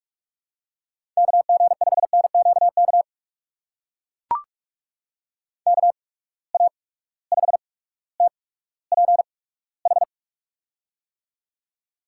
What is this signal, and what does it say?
Morse code, heard as a keyed tone, about 44 wpm, spaced as for 19 wpm, 700 Hz. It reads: KG5NQK KA5TPH
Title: USA Call Signs - 6 Characters - 8x Character Spacing - 45wpm